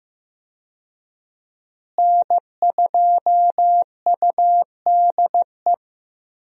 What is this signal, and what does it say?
Morse code, heard as a keyed tone, about 15 wpm, 700 Hz. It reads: N2UDE
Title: USA Call Signs - 5 Characters - Repeated 3x - 15wpm